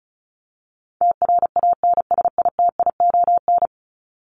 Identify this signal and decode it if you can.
Morse code, heard as a keyed tone, about 34 wpm, 700 Hz. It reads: TRANSITION
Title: Top 1751-2000 Words 35wpm